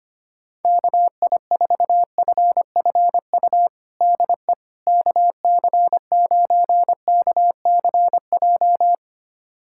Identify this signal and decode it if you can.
Morse code, heard as a keyed tone, about 25 wpm, 700 Hz. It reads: KI4FFU DE KC9KCJ
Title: USA Callsign DE Callsign 25wpm